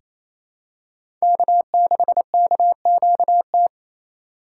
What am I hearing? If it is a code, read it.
Morse code, heard as a keyed tone, about 27 wpm, 700 Hz. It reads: K6KQT
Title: USA Call Signs - 5 Characters - 4x Character Spacing - 28wpm